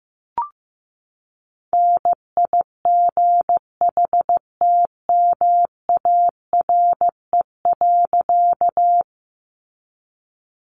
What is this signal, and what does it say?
Morse code, heard as a keyed tone, about 15 wpm, 700 Hz. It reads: NIGHTMARE.
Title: A Sound Of Thunder 15wpm